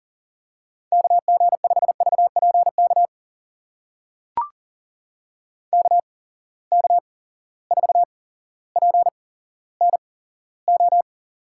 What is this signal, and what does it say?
Morse code, heard as a keyed tone, about 40 wpm, spaced as for 17 wpm, 700 Hz. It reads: KG5VPK KK4PNO
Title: USA Call Signs - 6 Characters - 8x Character Spacing - 40wpm